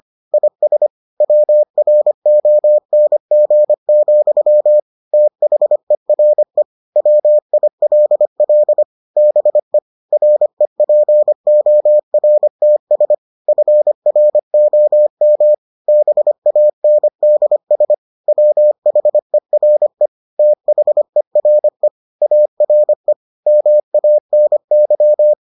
Morse code, heard as a keyed tone, about 25 wpm, 600 Hz. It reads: IS WRONG, THERE WILL BE REPORTS FROM BANDS WHERE THERE ARE MANY